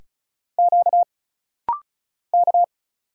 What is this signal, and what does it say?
Morse code, heard as a keyed tone, about 34 wpm, 700 Hz. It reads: Q K
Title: Single Letter-Number Alt - Rapid-Fire 35wpm